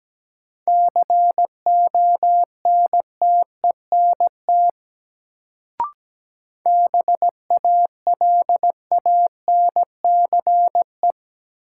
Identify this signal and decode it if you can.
Morse code, heard as a keyed tone, about 17 wpm, 700 Hz. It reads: CONTENT BALANCE
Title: Sentences from Top 900 Words - New Word Drill - 17wpm